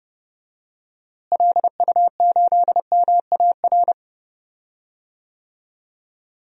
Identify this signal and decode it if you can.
Morse code, heard as a keyed tone, about 31 wpm, 700 Hz. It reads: LU8MAR
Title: DXCC-100 Call Signs - 6 Characters - Repeated 2x - 30wpm